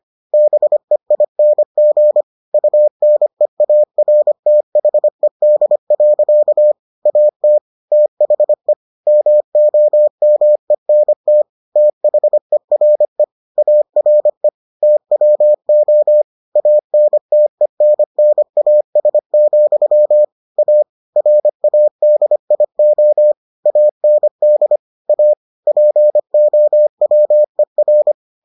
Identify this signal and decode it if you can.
Morse code, heard as a keyed tone, about 25 wpm, 600 Hz. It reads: BEING UNEARTHED. AT THE MOMENT THERE ARE TWO ANTENNAS, A RADIO AND A POWER